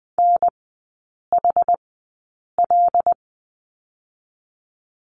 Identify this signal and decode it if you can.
Morse code, heard as a keyed tone, about 20 wpm, spaced as for 8 wpm, 700 Hz. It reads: NHL
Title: 107 Lesson 18e - Letter m - Three Characters 20wpm